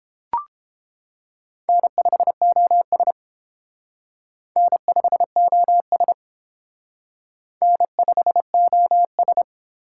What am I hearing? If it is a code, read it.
Morse code, heard as a keyed tone, about 29 wpm, 700 Hz. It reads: N5OS N5OS N5OS